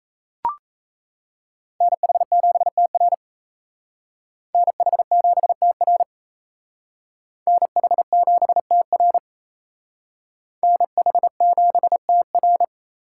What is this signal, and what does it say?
Morse code, heard as a keyed tone, about 35 wpm, 700 Hz. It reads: NH7TR NH7TR NH7TR NH7TR